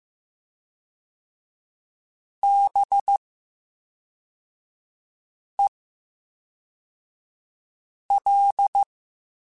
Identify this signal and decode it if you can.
Morse code, heard as a keyed tone, about 15 wpm, 800 Hz. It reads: B E L
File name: Special INGs / Special INGs-mp3s / VerySlowToFaster Ing @10